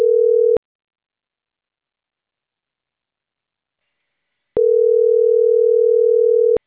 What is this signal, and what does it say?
Ringback tone (North America), 440 + 480 Hz.